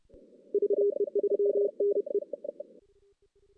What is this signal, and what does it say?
Morse code, heard as a keyed tone, about 30 wpm, 400 Hz. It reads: VE3NE
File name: 2016 JAN CW NAQP / 2016 Jan naqp k5ot - 0-5